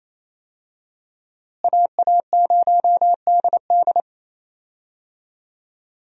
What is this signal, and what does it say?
Morse code, heard as a keyed tone, about 28 wpm, 700 Hz. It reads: AA0DD
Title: USA Call Signs - 5 Characters - Repeated 3x - 28wpm